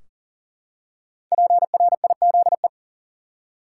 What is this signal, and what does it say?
Morse code, heard as a keyed tone, about 40 wpm, 700 Hz. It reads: PRIZE
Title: Five Letter Words - 40wpm